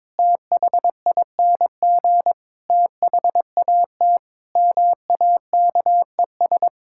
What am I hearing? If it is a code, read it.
Morse code, heard as a keyed tone, about 22 wpm, 700 Hz. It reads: THING THAT MAKES
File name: Sentences with 7-Word Phrases - 22wpm